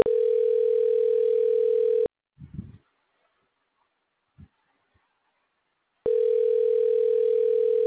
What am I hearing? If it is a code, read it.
Ringback tone (North America), 440 + 480 Hz.